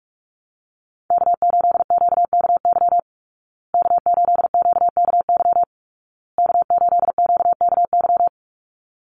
Morse code, heard as a keyed tone, about 47 wpm, 700 Hz. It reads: K8QKY K8QKY K8QKY
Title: Call Signs Repeated 2x - 45wpm